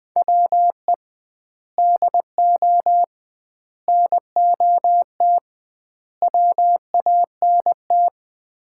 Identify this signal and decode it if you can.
Morse code, heard as a keyed tone, about 20 wpm, 700 Hz. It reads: WE DO NOT WANT
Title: 2x Word-Spacing - Sets of 4 Words - 20wpm